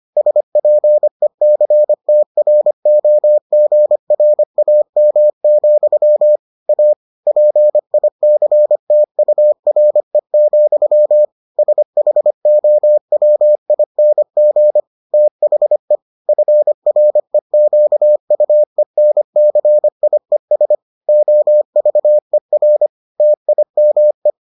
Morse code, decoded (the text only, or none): SPECTROGRAM, A PICTURE, SHOWING THE FREQUENCIES OVER TIME